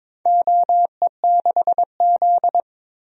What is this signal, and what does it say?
Morse code, heard as a keyed tone, about 22 wpm, 700 Hz. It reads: OE6Z
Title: DXCC-100 Call Signs - 4 Characters - Repeated 2x - 22wpm